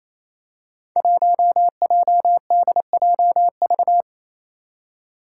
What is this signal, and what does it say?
Morse code, heard as a keyed tone, about 28 wpm, 700 Hz. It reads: 1JDJV
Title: Five Challenging Letter-Number Combinations 28wpm